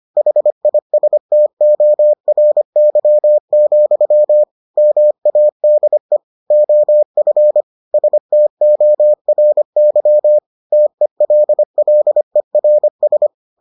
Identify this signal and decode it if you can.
Morse code, heard as a keyed tone, about 25 wpm, 600 Hz. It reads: HISTORY, MADE OF STORY TELLERS